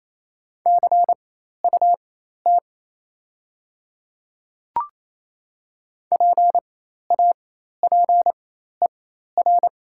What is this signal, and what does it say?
Morse code, heard as a keyed tone, about 27 wpm, spaced as for 13 wpm, 690 Hz. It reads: CUT PAPER
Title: 4x Character Spacing - Top 300 Words - 28wpm